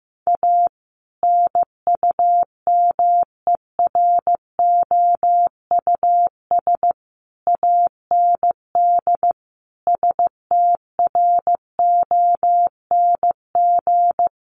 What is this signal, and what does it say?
Morse code, heard as a keyed tone, about 15 wpm, 700 Hz. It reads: A NUMEROUS AND STRONG
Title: Sentences from Top 1400 Words 15wpm